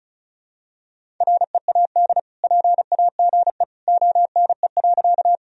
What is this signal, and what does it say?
Morse code, heard as a keyed tone, about 36 wpm, 700 Hz. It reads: READ PAGE ONE.
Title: Sentences from Top 1000 Words - Review Part 1 - 35wpm